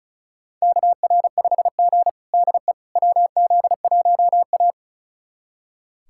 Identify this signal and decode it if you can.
Morse code, heard as a keyed tone, about 34 wpm, 700 Hz. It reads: KR5G DE WZ1A
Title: USA Callsign DE Callsign - 4 Characters - 35wpm